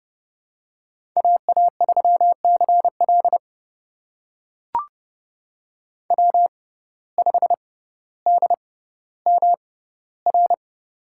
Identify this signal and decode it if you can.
Morse code, heard as a keyed tone, about 31 wpm, spaced as for 15 wpm, 700 Hz. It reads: AA3CL W5DMR
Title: USA Call Signs - 5 Characters - 6x Character Spacing - 30wpm